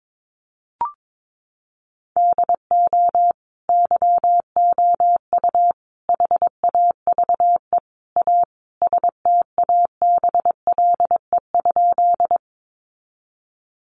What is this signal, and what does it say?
Morse code, heard as a keyed tone, about 22 wpm, 700 Hz. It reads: DO YOU HAVE A STABLE?
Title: Sentences from Top 1500 Words 22wpm